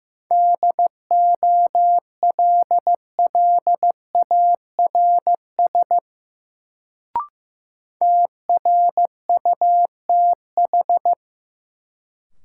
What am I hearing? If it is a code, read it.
Morse code, heard as a keyed tone, about 15 wpm, 700 Hz. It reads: DOLLARS TRUTH
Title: Sets of 5 Words - New Word Drill - 15wpm